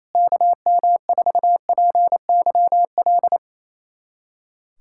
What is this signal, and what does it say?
Morse code, heard as a keyed tone, about 28 wpm, 700 Hz. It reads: KM4PYL
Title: USA Callsign DE Callsign 28wpm